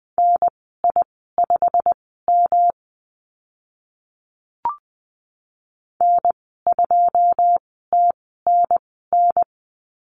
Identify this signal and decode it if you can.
Morse code, heard as a keyed tone, about 20 wpm, spaced as for 14 wpm, 700 Hz. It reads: NI5M N2TNN
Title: 161 Lesson 26g - Letter g - Call Signs 20wpm